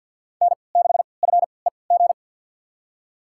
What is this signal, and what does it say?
Morse code, heard as a keyed tone, about 53 wpm, spaced as for 29 wpm, 700 Hz. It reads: N6FEG
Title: USA Call Signs - 5 Characters - 3x Character Spacing - 50wpm